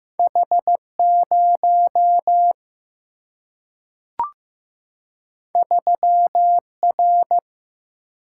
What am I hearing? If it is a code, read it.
Morse code, heard as a keyed tone, about 15 wpm, 700 Hz. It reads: H 0 3 R
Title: Single-Letter Single-Number 15wpm